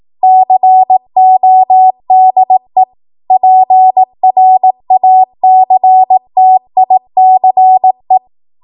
Morse code, heard as a keyed tone, about 18 wpm, 750 Hz. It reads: CODE PRACTICE
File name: Archive / 18 WPM / 18WPM